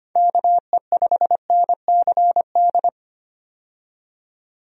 Morse code, heard as a keyed tone, about 25 wpm, 700 Hz. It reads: KE5NCD
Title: USA Call Signs - 6 Characters - 25wpm